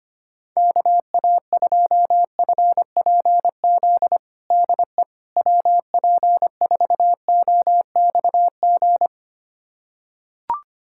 Morse code, heard as a keyed tone, about 25 wpm, 700 Hz. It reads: KA2FPZ DE WP4OXG
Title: USA Callsign DE Callsign - 6 Characters - 25wpm